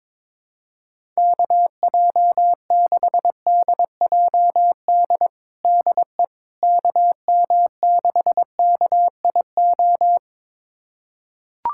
Morse code, heard as a keyed tone, about 22 wpm, 700 Hz. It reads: KJ6DJD DE KM6KIO
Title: USA Callsign DE Callsign 22wpm